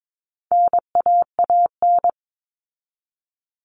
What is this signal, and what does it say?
Morse code, heard as a keyed tone, about 22 wpm, 700 Hz. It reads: NAAN
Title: All Cut Numbers - 4-Digit 22wpm